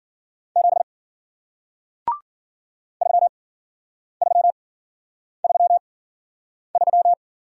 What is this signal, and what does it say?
Morse code, heard as a keyed tone, about 51 wpm, spaced as for 9 wpm, 700 Hz. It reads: 7 3333